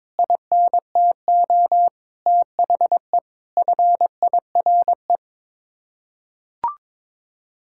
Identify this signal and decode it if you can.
Morse code, heard as a keyed tone, about 22 wpm, 700 Hz. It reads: INTO THE FIRE
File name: Sets of 9 Words - 22wpm